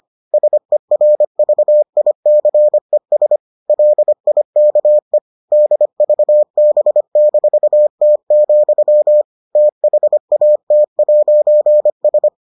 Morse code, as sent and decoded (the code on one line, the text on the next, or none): ... .-. ...- .. -.-. ... / .-.. .. -.- . / -.. ...- -... -....- - --..-- / - .... .- - .----. ...
SERVICES LIKE DVB-T, THAT'S